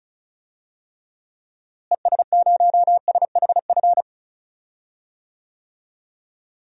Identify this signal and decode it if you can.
Morse code, heard as a keyed tone, about 36 wpm, 700 Hz. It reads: ES0SHF
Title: DXCC-100 Call Signs - 6 Characters - Repeated 3x - 35wpm